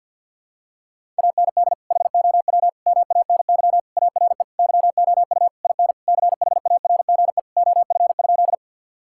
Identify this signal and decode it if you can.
Morse code, heard as a keyed tone, about 52 wpm, 700 Hz. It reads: AND HOW MANY ARE YOU IN CHARGE OF?
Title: Sentences from Top 1000 Words - Review Part 6 - 50wpm